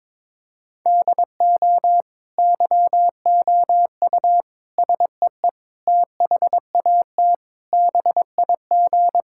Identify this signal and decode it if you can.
Morse code, heard as a keyed tone, about 22 wpm, 700 Hz. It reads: DO YOU SEE THAT BIG
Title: Sentences from Top 1000 Words - Review Part 5 - 22wpm